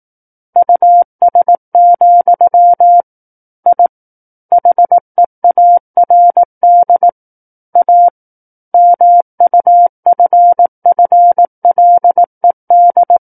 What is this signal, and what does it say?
Morse code, heard as a keyed tone, about 18 wpm, 700 Hz. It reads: US, I HEARD A MUFFLED
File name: 18 WPM